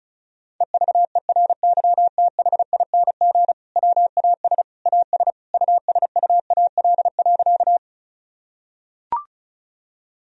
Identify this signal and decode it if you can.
Morse code, heard as a keyed tone, about 36 wpm, 700 Hz. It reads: EVERYTHING WAS AS USUAL.
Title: Sentences from Top 1000 Words - Review Part 2 - 35wpm